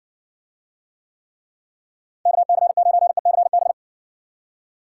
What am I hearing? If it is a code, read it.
Morse code, heard as a keyed tone, about 58 wpm, 700 Hz. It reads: KC9ECB